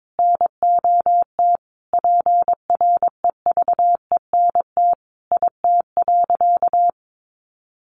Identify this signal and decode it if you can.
Morse code, heard as a keyed tone, about 22 wpm, 700 Hz. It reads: NOT PREVENT IT.
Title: Sentences from Top 1100 Words 22wpm